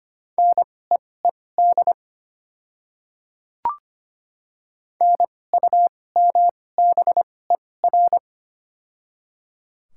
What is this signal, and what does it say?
Morse code, heard as a keyed tone, about 25 wpm, spaced as for 18 wpm, 700 Hz. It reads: NEED NUMBER
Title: 2x Character Spacing - Top 200 Words - 25wpm